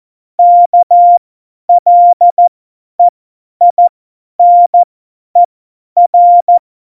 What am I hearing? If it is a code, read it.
Morse code, heard as a keyed tone, about 14 wpm, spaced as for 10 wpm, 700 Hz. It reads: KLEINER
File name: Morsekurs / Witze / Witz 03. Oktober 17, Tempo 50 F70